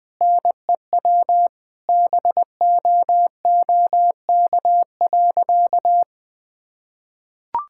Morse code, heard as a keyed tone, about 21 wpm, 700 Hz. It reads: NEW BOOK.
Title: Sentences with 3-Word Phrases - 20wpm